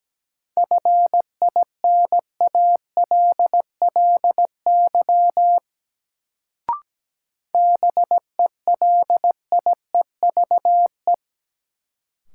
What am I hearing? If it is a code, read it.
Morse code, heard as a keyed tone, about 17 wpm, 700 Hz. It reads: FINALLY BELIEVE
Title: Sentences from Top 1000 Words - New Word Drill - 17wpm